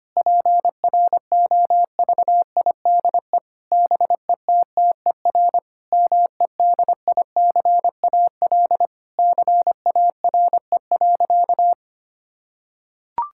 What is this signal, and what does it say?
Morse code, heard as a keyed tone, about 25 wpm, 700 Hz. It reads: PROVIDE BETTER MEDICAL CARE.
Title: Sentences from Top 1000 Words - Review Part 5 - 25wpm